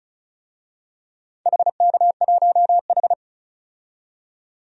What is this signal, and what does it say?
Morse code, heard as a keyed tone, about 36 wpm, 700 Hz. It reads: HK1H
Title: DXCC-100 Call Signs - 4 Characters - 35wpm